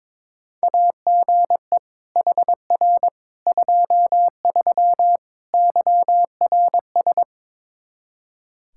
Morse code, heard as a keyed tone, about 22 wpm, 700 Hz. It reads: AGE HR 23 YRS
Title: Practice QSO - Exchange 3 - More Info - 22wpm